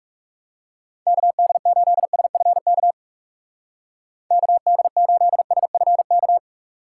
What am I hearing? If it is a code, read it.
Morse code, heard as a keyed tone, about 37 wpm, 700 Hz. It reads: KD8SFK KD8SFK
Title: Speed Racing - USA 2x3 Call Signs 30wpm